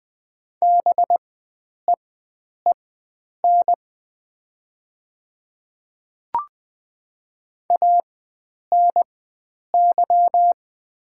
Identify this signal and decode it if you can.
Morse code, heard as a keyed tone, about 20 wpm, spaced as for 9 wpm, 700 Hz. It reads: BEEN ANY
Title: 4x Character Spacing - Top 500 Words - 20wpm